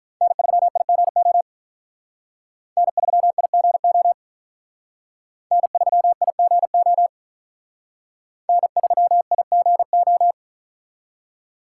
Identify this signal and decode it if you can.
Morse code, heard as a keyed tone, about 44 wpm, 700 Hz. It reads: N3IGO N3IGO N3IGO N3IGO
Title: Speed Racing - USA 1x3 Call Signs 35wpm